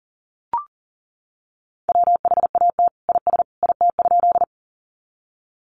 Morse code, heard as a keyed tone, about 37 wpm, 700 Hz. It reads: WHAT IS IT?
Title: Sentences from Top 100 Words 40wpm